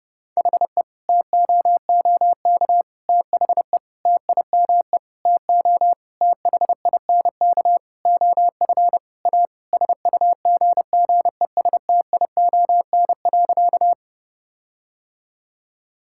Morse code, heard as a keyed tone, about 30 wpm, 700 Hz. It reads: HE TOOK THE TIME TO THINK OF A SUGGESTION.
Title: Sentences from Top 1600 Words 30wpm